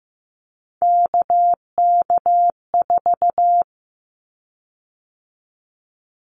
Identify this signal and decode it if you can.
Morse code, heard as a keyed tone, about 15 wpm, 700 Hz. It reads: KK4
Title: Callsign Prefixes for Top 25 DXCC - 15wpm